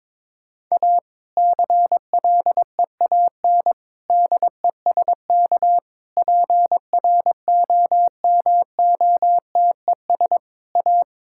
Morse code, heard as a keyed tone, about 21 wpm, 700 Hz. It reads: A CLEAN DESK PROMOTES A
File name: Difficult Sentences with Hints - 22wpm